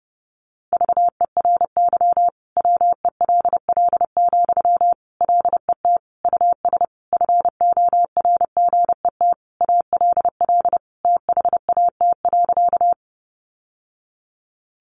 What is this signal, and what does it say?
Morse code, heard as a keyed tone, about 30 wpm, 700 Hz. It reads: VERY WELL, LET US FORGET ALL THAT.
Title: Sentences from Top 1200 Words 30wpm